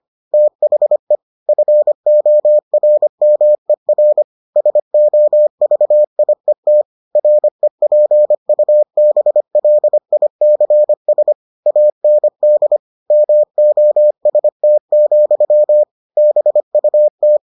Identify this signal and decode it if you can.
Morse code, heard as a keyed tone, about 25 wpm, 600 Hz. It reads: THE FORMER SOVIET REPUBLICS AND MOST, BUT